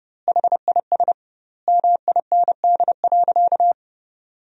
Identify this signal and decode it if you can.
Morse code, heard as a keyed tone, about 30 wpm, 700 Hz. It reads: HIS MIND.
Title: 2x Word-Spacing - Sentences from Top 500 Words - 30wpm